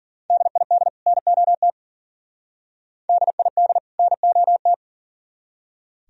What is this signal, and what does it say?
Morse code, heard as a keyed tone, about 52 wpm, 700 Hz. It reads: DID NOT DID NOT